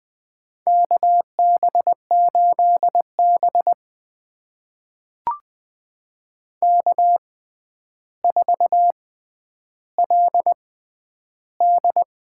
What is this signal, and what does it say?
Morse code, heard as a keyed tone, about 20 wpm, spaced as for 10 wpm, 700 Hz. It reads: KB8B K4LD
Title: USA Call Signs - 4 Characters - 6x Character Spacing - 20wpm